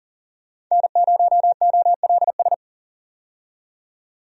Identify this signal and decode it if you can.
Morse code, heard as a keyed tone, about 40 wpm, 700 Hz. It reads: N0OLS
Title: USA Call Signs - 5 Characters - 40wpm